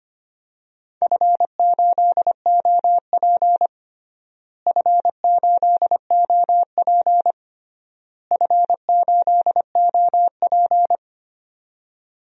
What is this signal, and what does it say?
Morse code, heard as a keyed tone, about 25 wpm, 700 Hz. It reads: F8OP F8OP F8OP
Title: DXCC-100 Call Signs - 4 Characters - Repeated 3x - 25wpm